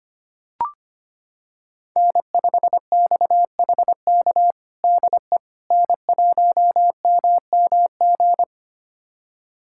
Morse code, heard as a keyed tone, about 25 wpm, 700 Hz. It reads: N5XHK DE N1MMG
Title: USA Callsign DE Callsign - 5 Characters - 25wpm